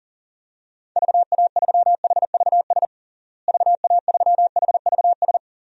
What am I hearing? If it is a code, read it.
Morse code, heard as a keyed tone, about 40 wpm, 700 Hz. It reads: VA3HVS VA3HVS
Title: DXCC-100 Call Signs - 6 Characters - Repeated 2x - 40wpm